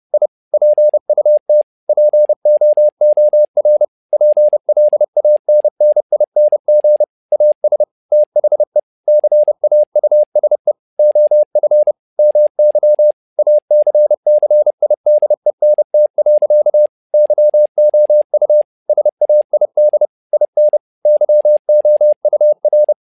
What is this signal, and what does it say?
Morse code, heard as a keyed tone, about 30 wpm, 600 Hz. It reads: I PUT POOR PLANNING AS THE CAUSE OF MY ACCIDENT. YOU SAID IN YOUR
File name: learning center / tower trouble / tower trouble 30wpm